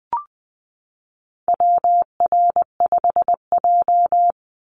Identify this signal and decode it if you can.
Morse code, heard as a keyed tone, about 20 wpm, 700 Hz. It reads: WR5J
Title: Call Signs - 20wpm